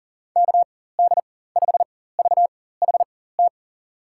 Morse code, heard as a keyed tone, about 40 wpm, spaced as for 19 wpm, 700 Hz. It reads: KD5VHT
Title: USA Call Signs - 6 Characters - 4x Character Spacing - 40wpm